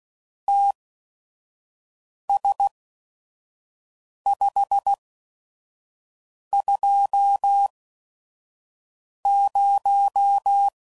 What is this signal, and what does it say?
Morse code, heard as a keyed tone, about 16 wpm, spaced as for 5 wpm, 800 Hz. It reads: TS520
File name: CW Code Groups @5WPM / QSOs / WD0FBT 2